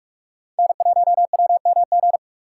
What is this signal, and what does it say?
Morse code, heard as a keyed tone, about 41 wpm, 700 Hz. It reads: N1WMG
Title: USA Call Signs - 3x Character Spacing - 45wpm